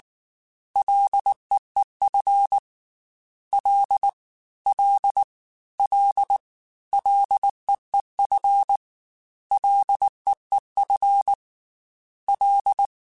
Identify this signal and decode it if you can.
Morse code, heard as a keyed tone, about 19 wpm, 800 Hz. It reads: LEEF L L L LEEF LEEF L
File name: Ls / Single L m3s / Single L Prefixe 3x2 Twice @18